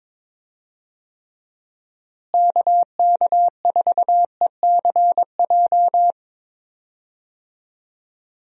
Morse code, heard as a keyed tone, about 22 wpm, 700 Hz. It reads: KK4ECJ
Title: USA Call Signs - 6 Characters - Repeated 3x - 22wpm